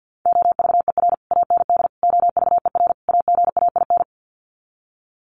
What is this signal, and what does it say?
Morse code, heard as a keyed tone, about 50 wpm, 700 Hz. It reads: OVER AND OVER AGAIN
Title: Sets of 4 Words 50wpm